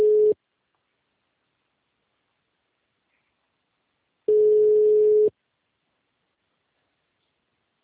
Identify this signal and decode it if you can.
Ringback tone (Europe, 425 Hz).